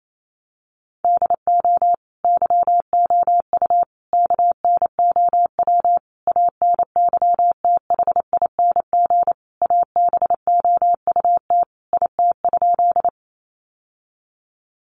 Morse code, heard as a keyed tone, about 28 wpm, 700 Hz. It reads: DO YOU KNOW ANYTHING ABOUT IT?
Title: Sentences from Top 1000 Words 28wpm